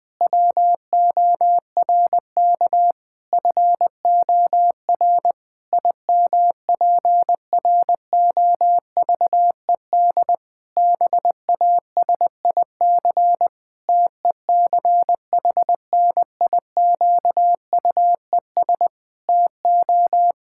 Morse code, heard as a keyed tone, about 20 wpm, 700 Hz. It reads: WORK FOR IMPROVED BASIC TECHNIQUES TO
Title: Pangram Challenge v2 - 20wpm